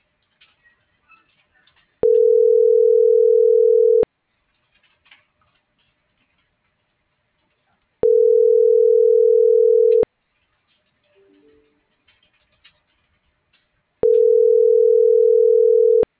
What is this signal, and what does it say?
Ringback tone (North America), 440 + 480 Hz.